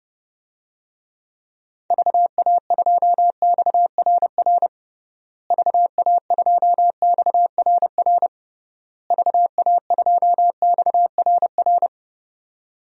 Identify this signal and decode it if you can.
Morse code, heard as a keyed tone, about 32 wpm, 700 Hz. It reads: VA2XRR VA2XRR VA2XRR
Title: DXCC-100 Call Signs - 6 Characters - Repeated 3x - 30wpm